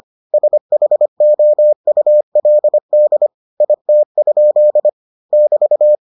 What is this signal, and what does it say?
Morse code, heard as a keyed tone, about 25 wpm, 600 Hz. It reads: SHOULD IT? <BT>